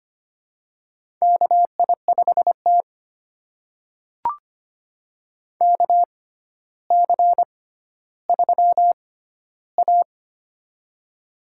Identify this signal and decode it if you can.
Morse code, heard as a keyed tone, about 25 wpm, spaced as for 13 wpm, 700 Hz. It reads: KI5T KC3A
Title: USA Call Signs - 4 Characters - 6x Character Spacing - 25wpm